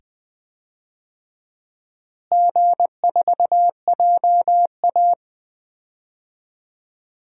Morse code, heard as a keyed tone, about 20 wpm, 700 Hz. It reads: G4JA
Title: DXCC-100 Call Signs - 4 Characters - Repeated 2x - 20wpm